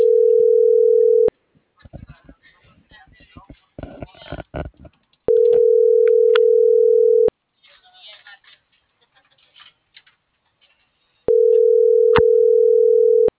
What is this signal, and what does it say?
Ringback tone (North America), 440 + 480 Hz.